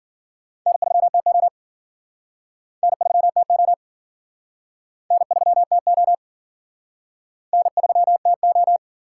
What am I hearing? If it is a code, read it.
Morse code, heard as a keyed tone, about 49 wpm, 700 Hz. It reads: N3TO N3TO N3TO N3TO
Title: Speed Racing - USA Call Signs 40wpm